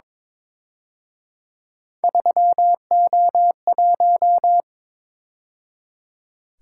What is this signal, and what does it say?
Morse code, heard as a keyed tone, about 28 wpm, 700 Hz. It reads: 3O1